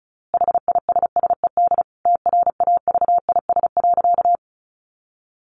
Morse code, heard as a keyed tone, about 34 wpm, 700 Hz. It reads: HISSED TRAVIS.
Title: A Sound Of Thunder 35wpm